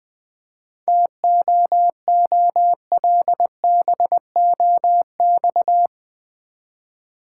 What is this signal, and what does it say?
Morse code, heard as a keyed tone, about 20 wpm, 690 Hz. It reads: TOOLBOX
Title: Easy Words with Hints - 20wpm